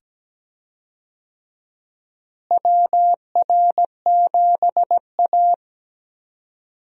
Morse code, heard as a keyed tone, about 17 wpm, 700 Hz. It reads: WR7A
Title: USA Call Signs - 4 Characters - Repeated 2x - 17wpm